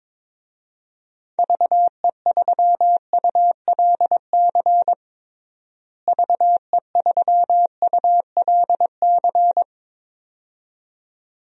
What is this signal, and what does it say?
Morse code, heard as a keyed tone, about 22 wpm, 700 Hz. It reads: VE3ULC VE3ULC
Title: DXCC-100 Call Signs - 6 Characters - Repeated 2x - 22wpm